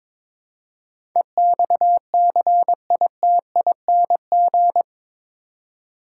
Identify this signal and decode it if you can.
Morse code, heard as a keyed tone, about 22 wpm, 700 Hz. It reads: EXCITING
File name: practice / 22wpm / Eight Letter Words - 22wpm